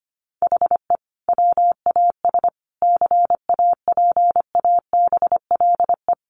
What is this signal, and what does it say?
Morse code, heard as a keyed tone, about 25 wpm, 700 Hz. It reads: HE WAS CAPABLE
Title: Sentences from Top 1200 Words 25wpm